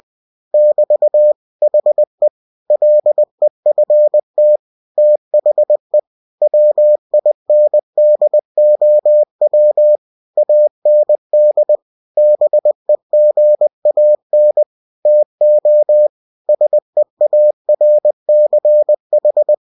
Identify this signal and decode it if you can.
Morse code, heard as a keyed tone, about 20 wpm, 600 Hz. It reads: <BT> HE LEFT THE WINDOW AND BEGAN TO SEARCH
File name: cw-books / winnie-the-pooh / winnie-20wpm-page-0092